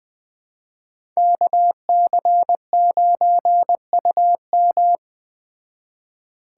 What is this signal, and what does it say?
Morse code, heard as a keyed tone, about 20 wpm, 700 Hz. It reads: KC9UM